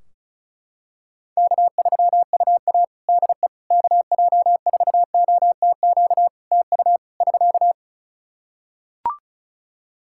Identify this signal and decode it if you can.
Morse code, heard as a keyed tone, about 36 wpm, 700 Hz. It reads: K3UA DE KJ4OTQ TU <SK>
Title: Practice QSO - Exchange 4 - Ending - 35wpm